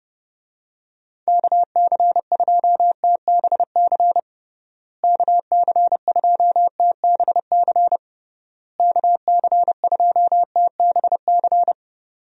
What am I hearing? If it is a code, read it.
Morse code, heard as a keyed tone, about 32 wpm, 700 Hz. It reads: KC2TBC KC2TBC KC2TBC
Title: USA Call Signs - 6 Characters - Repeated 3x - 30wpm